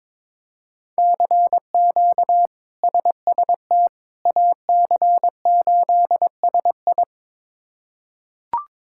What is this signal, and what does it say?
Morse code, heard as a keyed tone, about 22 wpm, 700 Hz. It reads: CQ SST AC8SI
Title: K1USN SST QSO Practice - 22wpm